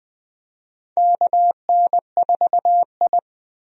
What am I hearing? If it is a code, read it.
Morse code, heard as a keyed tone, about 20 wpm, 700 Hz. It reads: KN4I
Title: USA Call Signs - 4 Characters - Repeated 3x - 20wpm